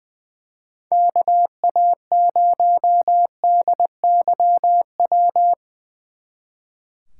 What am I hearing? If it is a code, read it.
Morse code, heard as a keyed tone, about 21 wpm, 700 Hz. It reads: KA0DYW
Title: Speed Racing - USA Call Signs 20wpm